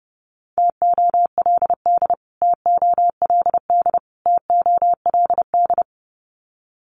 Morse code, heard as a keyed tone, about 30 wpm, 700 Hz. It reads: TOLD TOLD TOLD
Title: Top 500 Words Repeated 30wpm